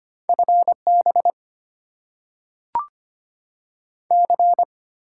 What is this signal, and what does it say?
Morse code, heard as a keyed tone, about 25 wpm, 690 Hz. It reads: FB C
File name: Top 20 QSO Elements - 6x Character Spacing - 25wpm